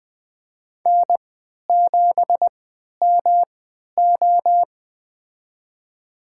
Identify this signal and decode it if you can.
Morse code, heard as a keyed tone, about 21 wpm, spaced as for 11 wpm, 700 Hz. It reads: N7MO
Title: USA Call Signs - 3x Character Spacing - 20wpm